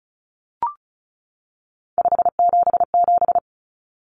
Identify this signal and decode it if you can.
Morse code, heard as a keyed tone, about 34 wpm, 700 Hz. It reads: 577
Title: Common Cut Numbers - 3-Digit 35wpm